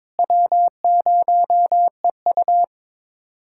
Morse code, heard as a keyed tone, about 22 wpm, 700 Hz. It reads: W0EU
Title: USA Callsign DE Callsign - 4 Characters - 22wpm